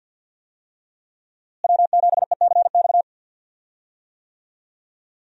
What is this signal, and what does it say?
Morse code, heard as a keyed tone, about 50 wpm, 700 Hz. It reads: W7ECX
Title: USA Call Signs - 5 Characters - Repeated 2x - 50wpm